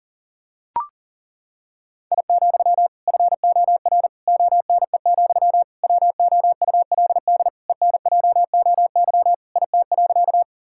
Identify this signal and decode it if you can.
Morse code, heard as a keyed tone, about 40 wpm, 700 Hz. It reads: I, FOR ONE, WOULD ENJOY IT.
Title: Sentences from Top 1000 Words - Review Part 5 - 40wpm